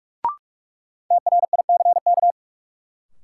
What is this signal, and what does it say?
Morse code, heard as a keyed tone, about 41 wpm, 700 Hz. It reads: TRICK
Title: Sentences from Top 1500 Words - New Word Drill Alt - 45wpm